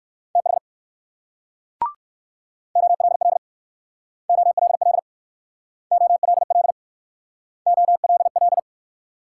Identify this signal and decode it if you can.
Morse code, heard as a keyed tone, about 60 wpm, 700 Hz. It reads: TH OLL OLL OLL OLL